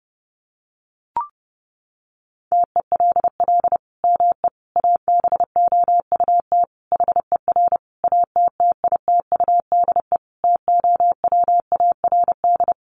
Morse code, heard as a keyed tone, about 30 wpm, 700 Hz. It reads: TELL ME ABOUT HER ATTITUDE TOWARD
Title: Sentences from Top 1300 Words 30wpm